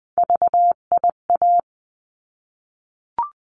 Morse code, heard as a keyed tone, about 20 wpm, 700 Hz. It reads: VIA
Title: 259 Lesson 42d - Words 18-20wpm